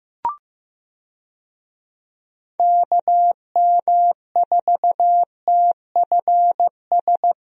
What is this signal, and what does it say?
Morse code, heard as a keyed tone, about 16 wpm, 700 Hz. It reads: KM4TFS
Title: USA Call Signs - 6 Characters - Repeated 3x - 15wpm